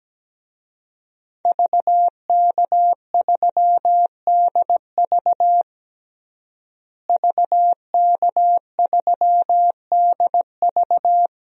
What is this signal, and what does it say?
Morse code, heard as a keyed tone, about 17 wpm, 700 Hz. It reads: VK3DV VK3DV
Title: DXCC-100 Call Signs - 5 Characters - Repeated 2x - 17wpm